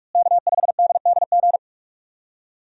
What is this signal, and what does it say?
Morse code, heard as a keyed tone, about 42 wpm, 700 Hz. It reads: K5DDG
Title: USA Call Signs - 5 Characters - Repeated 2x - 45wpm